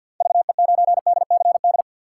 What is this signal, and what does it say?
Morse code, heard as a keyed tone, about 51 wpm, 690 Hz. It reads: VE9DCD
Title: K1USN SST QSO Practice - 50wpm